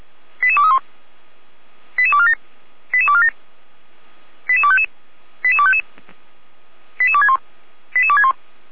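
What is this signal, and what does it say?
Five-tone selective call (ZVEI), heard as quick runs of five tones, tones of about 70 ms. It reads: 88231 88247 88247 88255 88255 88261 88261
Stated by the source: ZVEI five-tone selective call received on SDR.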